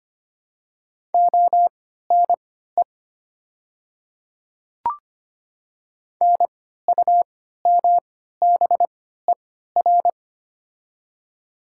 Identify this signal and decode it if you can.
Morse code, heard as a keyed tone, about 25 wpm, spaced as for 14 wpm, 700 Hz. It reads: ONE NUMBER
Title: 3x Character Spacing - Top 300 Words - 25wpm